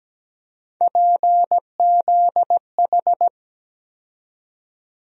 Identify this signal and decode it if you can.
Morse code, heard as a keyed tone, about 17 wpm, 700 Hz. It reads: PZH
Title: USA Call Sign Postfixes - 3 Letters 17wpm